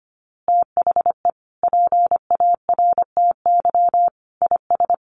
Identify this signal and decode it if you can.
Morse code, heard as a keyed tone, about 25 wpm, 700 Hz. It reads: THE PARTY IS
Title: Easy Phrases 25wpm